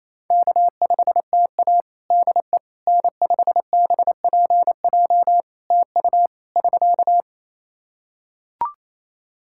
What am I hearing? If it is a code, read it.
Morse code, heard as a keyed tone, about 28 wpm, 700 Hz. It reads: K5TA DE N5BPJ TU <SK>
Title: K1USN SST QSO Practice 28wpm